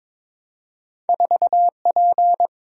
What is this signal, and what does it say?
Morse code, heard as a keyed tone, about 22 wpm, 700 Hz. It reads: 4P
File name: practice / 22wpm / Two Letter-Number Combinations - 22wpm